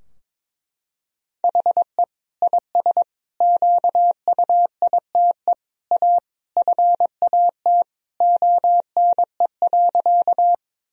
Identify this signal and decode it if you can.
Morse code, heard as a keyed tone, about 22 wpm, 700 Hz. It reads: HE IS QUITE A FAT ONE.
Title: Sentences from Top 1000 Words - Review Part 4 - 22wpm